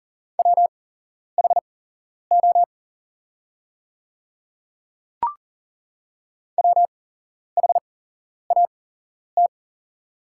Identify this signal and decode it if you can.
Morse code, heard as a keyed tone, about 39 wpm, spaced as for 11 wpm, 700 Hz. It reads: WHO WHAT
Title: 8x Character Spacing - Top 100 Words - 40wpm